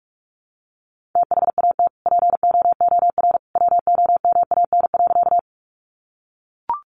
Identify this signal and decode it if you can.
Morse code, heard as a keyed tone, about 44 wpm, 700 Hz. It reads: THAT POOR WOMAN.
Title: Sentences from Top 1100 Words 45wpm